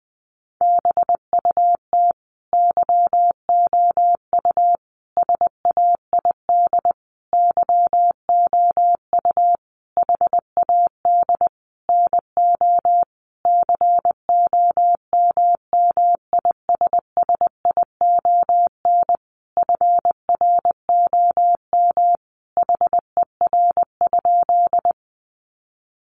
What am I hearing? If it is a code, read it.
Morse code, heard as a keyed tone, about 20 wpm, 700 Hz. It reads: BUT YOU SAID YOU HAD NO COMMISSION FROM HER?
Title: Sentences from Top 1400 Words 20wpm